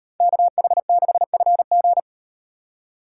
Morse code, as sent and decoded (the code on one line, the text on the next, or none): -.- .... -.... ..-. --.
KH6FG